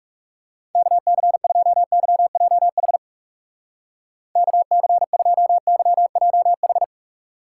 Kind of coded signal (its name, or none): Morse code